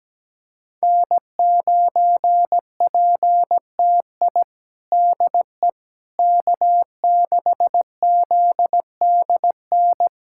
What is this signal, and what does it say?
Morse code, heard as a keyed tone, about 17 wpm, 700 Hz. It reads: N9PTI DE K6ZDN
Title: USA Callsign DE Callsign - 5 Characters - 17wpm